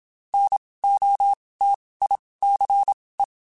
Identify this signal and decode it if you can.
Morse code, heard as a keyed tone, about 27 wpm, spaced as for 19 wpm, 800 Hz. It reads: NOTICE
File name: QSOs / QSOs-mp3s / WD0FBT 1@25